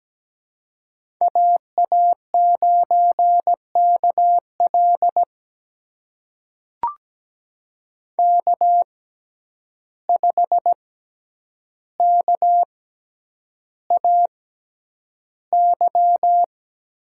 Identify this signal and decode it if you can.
Morse code, heard as a keyed tone, about 17 wpm, spaced as for 9 wpm, 700 Hz. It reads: AA9KL K5KAY
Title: USA Call Signs - 5 Characters - 6x Character Spacing - 17wpm